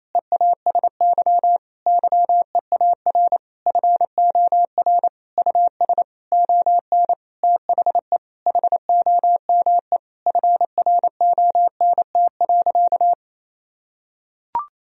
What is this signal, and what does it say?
Morse code, heard as a keyed tone, about 28 wpm, 700 Hz. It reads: EASY YEAR FOR US ON THE HOME FRONT.
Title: Sentences from Top 1000 Words - Review Part 2 - 28wpm